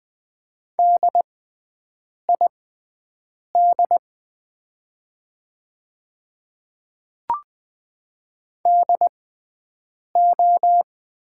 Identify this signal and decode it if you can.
Morse code, heard as a keyed tone, about 20 wpm, spaced as for 7 wpm, 700 Hz. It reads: DID DO
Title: 6x Character Spacing - Top 100 Words - 20wpm